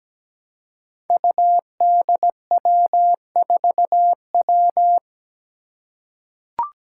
Morse code, heard as a keyed tone, about 17 wpm, 700 Hz. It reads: UDW4W
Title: Five Challenging Letter-Number Combinations 17wpm